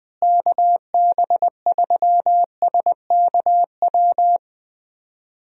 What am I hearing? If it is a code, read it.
Morse code, heard as a keyed tone, about 21 wpm, 700 Hz. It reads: KB3SKW
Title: USA Call Signs - 6 Characters - Repeated 3x - 20wpm